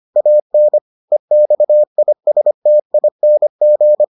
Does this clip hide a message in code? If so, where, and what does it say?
Yes: Morse code from 0.2 to 4.0 s, heard as a keyed tone, about 25 wpm, 600 Hz. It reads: AN EXISTING